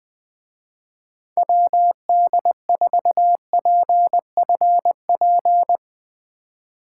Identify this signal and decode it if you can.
Morse code, heard as a keyed tone, about 22 wpm, 700 Hz. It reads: WD4PFP
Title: Speed Racing - USA 2x3 Call Signs 20wpm